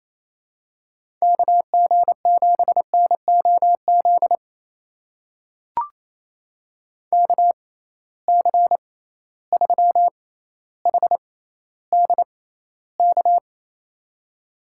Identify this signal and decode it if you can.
Morse code, heard as a keyed tone, about 28 wpm, spaced as for 14 wpm, 700 Hz. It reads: KG7NOZ KC3HDK
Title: USA Call Signs - 6 Characters - 6x Character Spacing - 28wpm